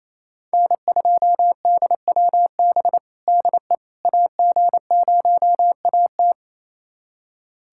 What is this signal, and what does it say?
Morse code, heard as a keyed tone, about 27 wpm, 700 Hz. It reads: N2DWB DE AG0AT
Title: USA Callsign DE Callsign - 5 Characters - 28wpm